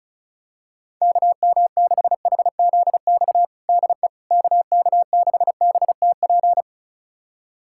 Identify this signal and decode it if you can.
Morse code, heard as a keyed tone, about 34 wpm, 700 Hz. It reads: KM6HZX DE KK6BTP
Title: USA Callsign DE Callsign 35wpm